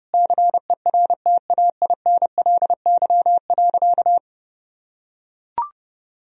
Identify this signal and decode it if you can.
Morse code, heard as a keyed tone, about 32 wpm, 700 Hz. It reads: CERTAINLY.
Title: Sentences from Top 1000 Words - Review Part 1 - 30wpm